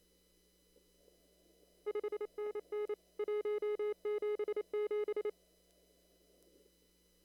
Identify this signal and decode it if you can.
Morse code, heard as a keyed tone, about 25 wpm, 410 Hz. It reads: 5NN 177
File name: CW / kt5j wpx cw-op-k5pi